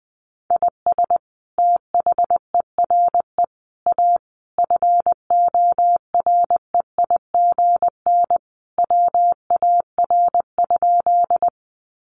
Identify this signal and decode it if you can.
Morse code, heard as a keyed tone, about 20 wpm, 700 Hz. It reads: IS THERE A FOREIGN WAR?
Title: Sentences from Top 1400 Words 20wpm